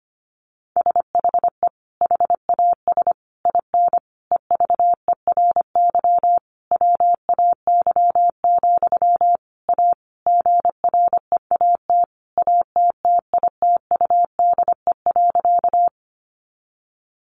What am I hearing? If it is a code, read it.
Morse code, heard as a keyed tone, about 25 wpm, 700 Hz. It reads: SHE HAS IN EVERY WAY, A GREAT ATTITUDE.
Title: Sentences from Top 1300 Words 25wpm